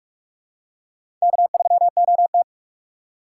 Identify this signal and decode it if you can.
Morse code, heard as a keyed tone, about 42 wpm, 700 Hz. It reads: K3OT
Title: USA Call Signs - 2x Character Spacing - 45wpm